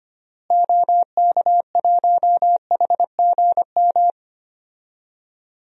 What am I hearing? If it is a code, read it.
Morse code, heard as a keyed tone, about 25 wpm, 700 Hz. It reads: OK1HGM
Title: DXCC-100 Call Signs - 6 Characters - Repeated 2x - 25wpm